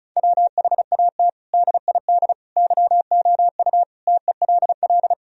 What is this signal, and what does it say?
Morse code, heard as a keyed tone, about 36 wpm, 700 Hz. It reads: WHAT DID YOU TELL
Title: Sentences from Top 1000 Words - Review Part 4 - 35wpm